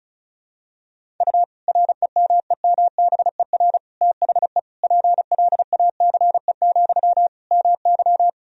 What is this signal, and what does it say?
Morse code, heard as a keyed tone, about 36 wpm, 700 Hz. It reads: U REMEMBER THE PLACE, MY
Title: Sentences from Top 400 Words and Abbreviations 35wpm